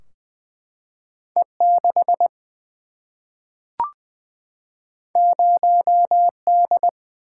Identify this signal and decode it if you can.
Morse code, heard as a keyed tone, about 20 wpm, 700 Hz. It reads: E6 0D
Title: Single-Letter Single-Number 20wpm